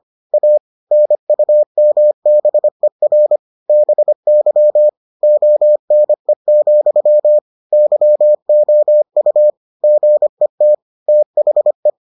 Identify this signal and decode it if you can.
Morse code, heard as a keyed tone, about 25 wpm, 600 Hz. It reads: A NUMBER BY ONE, YOU GET THE